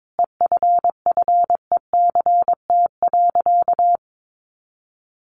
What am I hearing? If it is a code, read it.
Morse code, heard as a keyed tone, about 22 wpm, 700 Hz. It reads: EFFECT.
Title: Sentences from Top 1000 Words 22wpm